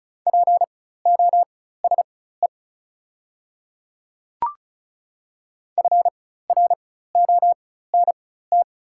Morse code, heard as a keyed tone, about 36 wpm, spaced as for 16 wpm, 700 Hz. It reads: POSE FRONT
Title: 4x Character Spacing - Top 500 Words - 35wpm